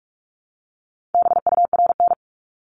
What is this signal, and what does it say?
Morse code, heard as a keyed tone, about 46 wpm, 700 Hz. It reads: BURN
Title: Multi-letter Phonemes Repeated 3x 45wpm